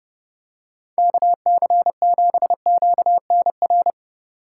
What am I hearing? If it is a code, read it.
Morse code, heard as a keyed tone, about 30 wpm, 700 Hz. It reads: KC7QNR
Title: USA Call Signs - 2x Character Spacing - 30wpm